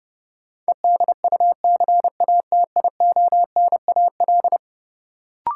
Morse code, heard as a keyed tone, about 32 wpm, 700 Hz. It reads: EDUCATIONAL
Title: Sentences from Top 1200 Words - New Word Drill - 30wpm